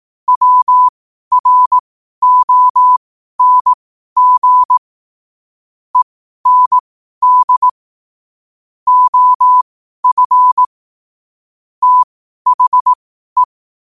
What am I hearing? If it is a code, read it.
Morse code, heard as a keyed tone, about 18 wpm, spaced as for 12 wpm, 1000 Hz. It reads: WRONG END OF THE
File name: qotdfiles / 12wpm